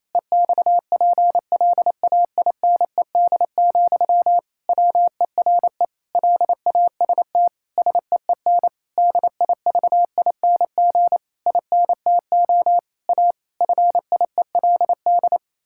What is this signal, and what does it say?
Morse code, heard as a keyed tone, about 27 wpm, 700 Hz. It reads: EXPLAINED, WERE LAST SEEN DIVING INTO A FIELD
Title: Pangram Challenge v2 - 28wpm